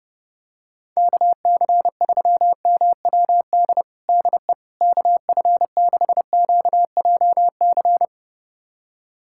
Morse code, heard as a keyed tone, about 32 wpm, 700 Hz. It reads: KC3MWD DE KF6QJC
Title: USA Callsign DE Callsign 30wpm